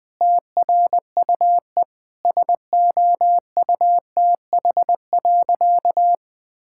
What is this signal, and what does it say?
Morse code, heard as a keyed tone, about 20 wpm, 700 Hz. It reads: TRUE SOUTH.
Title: Sentences from Top 1000 Words - Review Part 6 - 20wpm